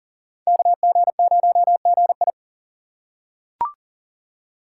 Morse code, heard as a keyed tone, about 39 wpm, 700 Hz. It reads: KG0GI
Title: USA Call Signs - 5 Characters - 2x Character Spacing - 40wpm